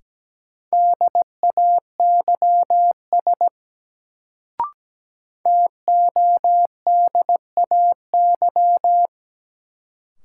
Morse code, heard as a keyed tone, about 17 wpm, 700 Hz. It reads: DAYS TODAY
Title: Sets of 5 Words Encore - New Word Drill - 17wpm